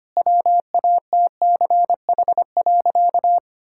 Morse code, heard as a keyed tone, about 25 wpm, 700 Hz. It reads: WATCH.
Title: Sentences from Top 1000 Words - Review Part 5 - 25wpm